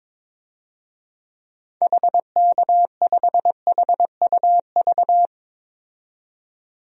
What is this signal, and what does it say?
Morse code, heard as a keyed tone, about 22 wpm, 700 Hz. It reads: HK5HUV
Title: DXCC-100 Call Signs - 6 Characters - 22wpm